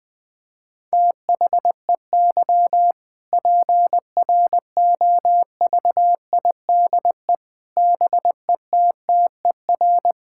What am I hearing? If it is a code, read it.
Morse code, heard as a keyed tone, about 20 wpm, 700 Hz. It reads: THEY PROVIDE BETTER
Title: Sentences from Top 1000 Words - Review Part 5 - 20wpm